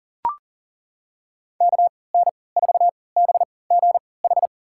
Morse code, heard as a keyed tone, about 39 wpm, spaced as for 23 wpm, 700 Hz. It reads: KN4BGH
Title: USA Call Signs - 6 Characters - 3x Character Spacing - 40wpm